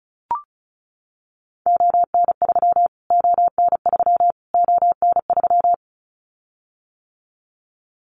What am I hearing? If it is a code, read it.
Morse code, heard as a keyed tone, about 34 wpm, 700 Hz. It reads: ON3 ON3 ON3
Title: Callsign Prefixes for Top 25 DXCC - Repeated 3x - 35wpm